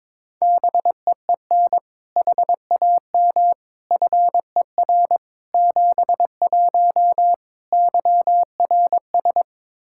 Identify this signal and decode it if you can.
Morse code, heard as a keyed tone, about 22 wpm, 700 Hz. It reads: BEEN HAM FER 71 YRS
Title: Practice QSO - Exchange 3 - More Info - 22wpm